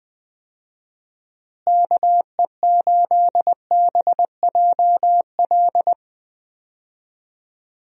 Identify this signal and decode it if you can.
Morse code, heard as a keyed tone, about 20 wpm, 700 Hz. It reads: KE8BJL
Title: USA Call Signs - 6 Characters - Repeated 2x - 20wpm